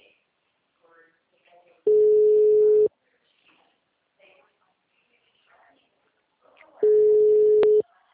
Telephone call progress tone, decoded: ringback tone

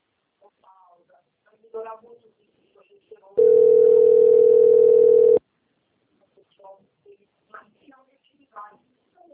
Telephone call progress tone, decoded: ringback tone

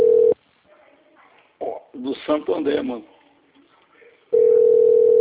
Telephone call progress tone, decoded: ringback tone

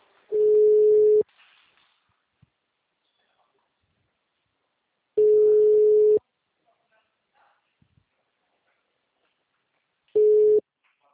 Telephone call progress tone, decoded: ringback tone

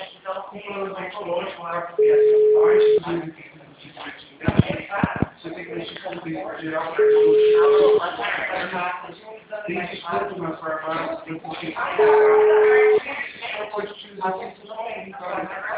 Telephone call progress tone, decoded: ringback tone

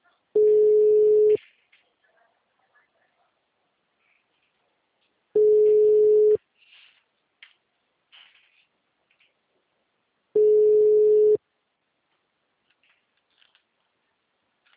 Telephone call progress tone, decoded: ringback tone

